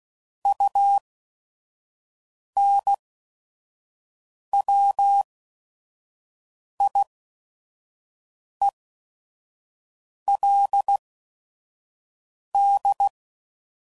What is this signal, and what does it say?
Morse code, heard as a keyed tone, about 16 wpm, spaced as for 5 wpm, 800 Hz. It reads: UNWIELD
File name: CW Code Groups @5WPM / Technical Subjects / AND Gate